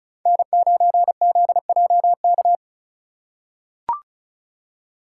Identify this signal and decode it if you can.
Morse code, heard as a keyed tone, about 36 wpm, 700 Hz. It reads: N9ZJK
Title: USA Call Signs - 5 Characters - 2x Character Spacing - 35wpm